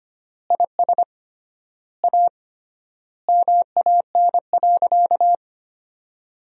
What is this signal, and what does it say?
Morse code, heard as a keyed tone, about 25 wpm, 700 Hz. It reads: IS A MAN.
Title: 3x Word-Spacing - Sentences from Top 500 Words - 25wpm